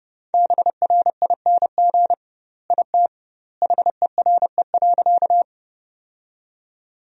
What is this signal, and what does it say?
Morse code, heard as a keyed tone, about 30 wpm, 700 Hz. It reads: BRING IT HERE.
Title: 2x Word-Spacing - Sentences from Top 500 Words - 30wpm